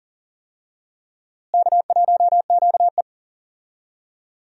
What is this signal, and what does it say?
Morse code, heard as a keyed tone, about 39 wpm, 700 Hz. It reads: K1QE
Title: USA Call Signs - 4 Characters - Repeated 2x - 40wpm